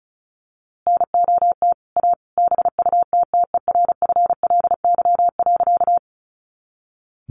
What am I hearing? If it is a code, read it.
Morse code, heard as a keyed tone, about 34 wpm, 700 Hz. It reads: NOT A BUTTERFLY.
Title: A Sound Of Thunder 35wpm